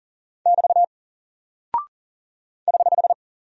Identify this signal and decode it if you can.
Morse code, heard as a keyed tone, about 40 wpm, 700 Hz. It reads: <BT> <HH>
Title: Single Character (Letter:Number:Punctuation:Prosign) 40wpm